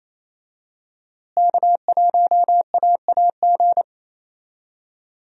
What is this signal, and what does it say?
Morse code, heard as a keyed tone, about 28 wpm, 700 Hz. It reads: K1AAG
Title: USA Call Signs - 5 Characters - Repeated 3x - 28wpm